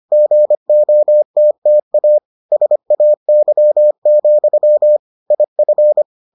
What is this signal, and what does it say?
Morse code, heard as a keyed tone, about 25 wpm, 600 Hz. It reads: GOTTA SAY, IF